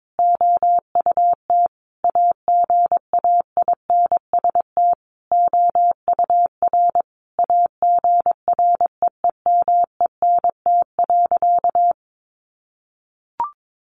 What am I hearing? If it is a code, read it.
Morse code, heard as a keyed tone, about 22 wpm, 700 Hz. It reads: OUT AGAINST OUR AGREEMENT.
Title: Sentences from Top 1500 Words 22wpm